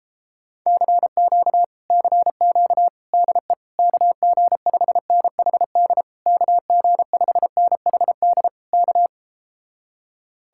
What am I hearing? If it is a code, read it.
Morse code, heard as a keyed tone, about 37 wpm, 700 Hz. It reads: CQ CQ DE KG5NHD KG5NHD K